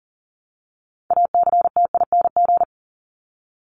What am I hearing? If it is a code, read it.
Morse code, heard as a keyed tone, about 40 wpm, 700 Hz. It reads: ACTING
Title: ING Suffix - Top 100 Words with Base Words 40wpm